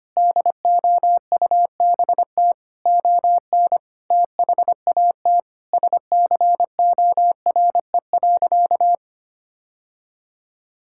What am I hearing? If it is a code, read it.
Morse code, heard as a keyed tone, about 25 wpm, 700 Hz. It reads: DOUBT ON THAT SCORE.
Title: Sentences from Top 1600 Words 25wpm